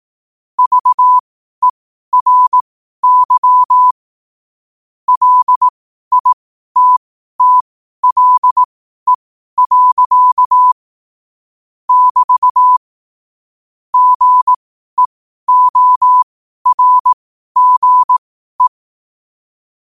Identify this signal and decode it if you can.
Morse code, heard as a keyed tone, about 18 wpm, spaced as for 12 wpm, 1000 Hz. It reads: VERY LITTLE. <BT> GEORGE
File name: qotdfiles / 12wpm